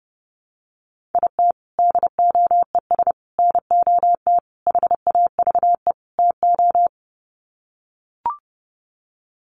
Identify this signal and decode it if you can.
Morse code, heard as a keyed tone, about 30 wpm, 700 Hz. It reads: IT DOES NOT HAVE TO
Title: Sets of 5 Words 30wpm